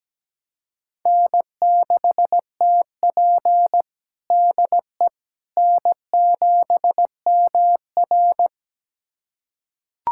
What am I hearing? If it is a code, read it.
Morse code, heard as a keyed tone, about 17 wpm, 700 Hz. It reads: N6TP DE N7MR
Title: USA Callsign DE Callsign - 4 Characters - 17wpm